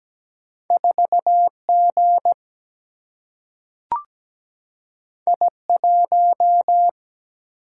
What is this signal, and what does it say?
Morse code, heard as a keyed tone, about 17 wpm, 700 Hz. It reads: 4G I1
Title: Single-Letter Single-Number 17wpm